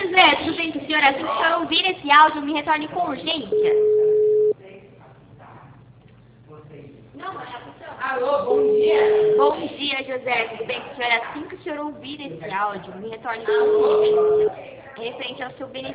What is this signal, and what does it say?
Ringback tone (Europe, 425 Hz), with speech.